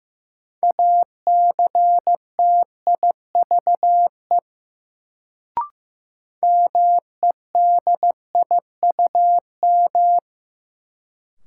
Morse code, heard as a keyed tone, about 15 wpm, 700 Hz. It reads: ACTIVE MEDIUM
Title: Sentences from Top 900 Words - New Word Drill - 15wpm